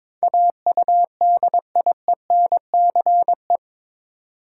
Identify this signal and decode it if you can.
Morse code, heard as a keyed tone, about 22 wpm, 700 Hz. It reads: AUDIENCE
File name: practice / 22wpm / Eight Letter Words - 22wpm